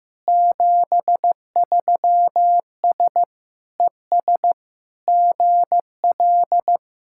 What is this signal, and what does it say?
Morse code, heard as a keyed tone, about 15 wpm, 690 Hz. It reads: 73S ES GL
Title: SOTA QSO Practice 15wpm